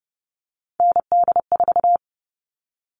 Morse code, heard as a keyed tone, about 30 wpm, 700 Hz. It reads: ND4
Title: All Cut Numbers - 3-Digit 30wpm